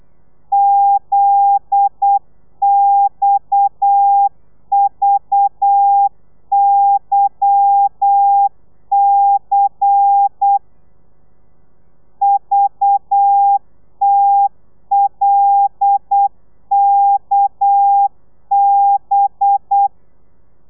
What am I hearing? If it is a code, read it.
Morse code, heard as a keyed tone, about 8 wpm, 800 Hz. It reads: ZXVYC VTLKB
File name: telegrafia / i2tfj / Lez10 35